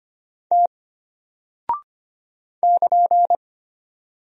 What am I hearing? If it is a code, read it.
Morse code, heard as a keyed tone, about 25 wpm, 700 Hz. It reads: T <KN>
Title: Single Character (Letter:Number:Punctuation:Prosign) Alt 25wpm